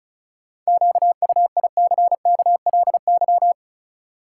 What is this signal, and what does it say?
Morse code, heard as a keyed tone, about 34 wpm, 700 Hz. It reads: QUICKLY
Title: Sentences from Top 700 Words - New Word Drill - 35wpm